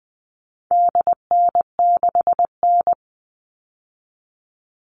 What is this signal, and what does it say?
Morse code, heard as a keyed tone, about 20 wpm, 700 Hz. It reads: DN6N
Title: All Cut Numbers - 4-Digit 20wpm